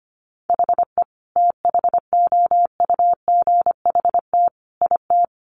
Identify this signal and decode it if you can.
Morse code, heard as a keyed tone, about 25 wpm, 700 Hz. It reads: HE THOUGHT IT
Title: Sentences from Top 1200 Words 25wpm